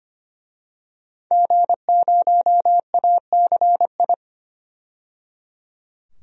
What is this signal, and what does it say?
Morse code, heard as a keyed tone, about 25 wpm, 700 Hz. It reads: G0ACI
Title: DXCC-100 Call Signs - 5 Characters - 25wpm